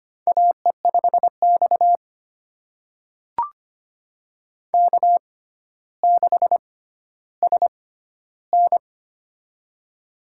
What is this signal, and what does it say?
Morse code, heard as a keyed tone, about 25 wpm, spaced as for 13 wpm, 700 Hz. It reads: AE5X K6SN
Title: USA Call Signs - 4 Characters - 6x Character Spacing - 25wpm